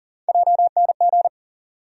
Morse code, heard as a keyed tone, about 40 wpm, 700 Hz. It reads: JNG
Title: USA Call Sign Postfixes - 3 Letters 40wpm